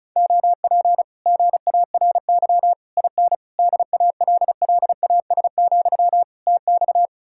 Morse code, heard as a keyed tone, about 40 wpm, 700 Hz. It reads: OP GARY IN DALLAS, TX